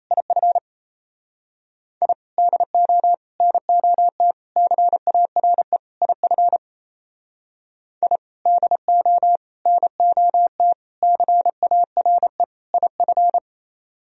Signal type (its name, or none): Morse code